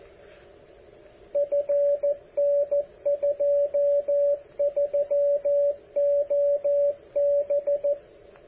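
Morse code, heard as a keyed tone, about 14 wpm, 570 Hz, with radio noise. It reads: FN23OB
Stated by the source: KC4QLP 10 meter beacon, transmitter built by AA7DJ, power just over 10 watts.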